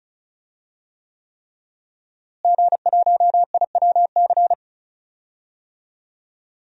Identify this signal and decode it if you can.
Morse code, heard as a keyed tone, about 36 wpm, 700 Hz. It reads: G1IWC
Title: DXCC-100 Call Signs - 5 Characters - Repeated 3x - 35wpm